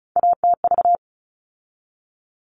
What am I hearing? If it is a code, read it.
Morse code, heard as a keyed tone, about 34 wpm, 700 Hz. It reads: ATV
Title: All Cut Numbers - 3-Digit 35wpm